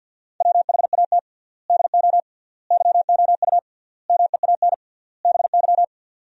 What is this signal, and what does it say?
Morse code, heard as a keyed tone, about 50 wpm, 700 Hz. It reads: WHAT DO YOU MEAN BY
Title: 3x Word-Spacing - Sentences from Top 400 Words - 50wpm